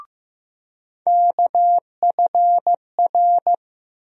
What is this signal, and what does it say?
Morse code, heard as a keyed tone, about 15 wpm, 700 Hz. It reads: KFR
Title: USA Call Sign Postfixes - 3 Letters 15wpm